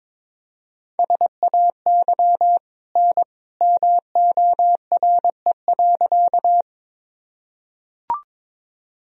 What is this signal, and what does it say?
Morse code, heard as a keyed tone, about 22 wpm, 700 Hz. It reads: SAY N MORE.
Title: Sentences from Top 200 Words and Abbreviations 22wpm